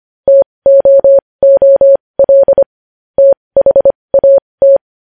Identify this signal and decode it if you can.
Morse code, heard as a keyed tone, about 25 wpm, spaced as for 20 wpm, 550 Hz. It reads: TOOL THAT
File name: podcast / News headlines-1x-25x20